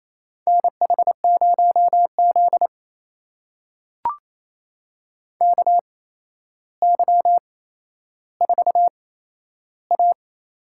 Morse code, heard as a keyed tone, about 28 wpm, spaced as for 12 wpm, 700 Hz. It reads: NH0Z KY4A
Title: USA Call Signs - 4 Characters - 8x Character Spacing - 28wpm